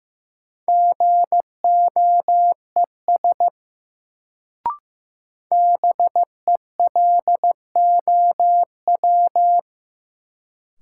Morse code, heard as a keyed tone, about 15 wpm, 700 Hz. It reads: GOES BELOW
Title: Sentences from Top 1000 Words - Review Part 5 - New Word Drill - 15wpm